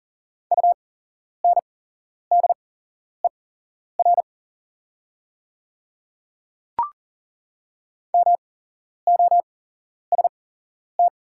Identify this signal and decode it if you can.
Morse code, heard as a keyed tone, about 40 wpm, spaced as for 11 wpm, 690 Hz. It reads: UNDER MOST
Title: 8x Character Spacing - Top 300 Words - 40wpm